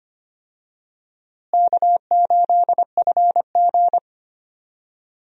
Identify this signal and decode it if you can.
Morse code, heard as a keyed tone, about 25 wpm, 700 Hz. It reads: K8FG
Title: USA Call Signs - 4 Characters - Repeated 3x - 25wpm